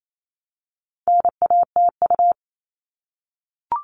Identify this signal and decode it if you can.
Morse code, heard as a keyed tone, about 28 wpm, 700 Hz. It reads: NATU
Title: All Cut Numbers - 4-Digit 28wpm